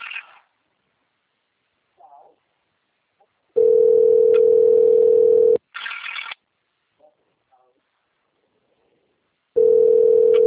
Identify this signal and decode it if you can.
Ringback tone (North America), 440 + 480 Hz.